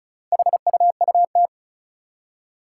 Morse code, heard as a keyed tone, about 36 wpm, 700 Hz. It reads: HUUT
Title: Four Challenging Letter-Number Combinations 35wpm